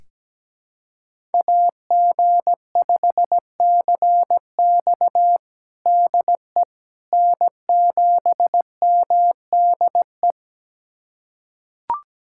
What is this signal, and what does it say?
Morse code, heard as a keyed tone, about 17 wpm, 700 Hz. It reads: AG5CX DE N7MDE
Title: USA Callsign DE Callsign - 5 Characters - 17wpm